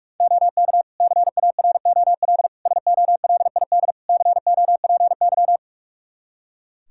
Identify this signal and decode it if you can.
Morse code, heard as a keyed tone, about 46 wpm, 700 Hz. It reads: OK CAROL SOLID COPY
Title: K1USN SST QSO Practice 45wpm